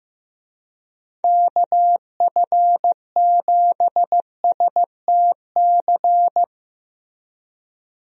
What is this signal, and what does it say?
Morse code, heard as a keyed tone, about 15 wpm, 700 Hz. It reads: KF7STC
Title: USA Call Signs - 6 Characters - Repeated 3x - 15wpm